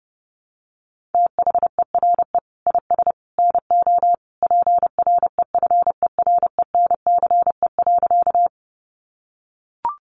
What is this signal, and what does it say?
Morse code, heard as a keyed tone, about 30 wpm, 700 Hz. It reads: THERE IS NO PREFERENCE.
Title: Sentences from Top 1500 Words 30wpm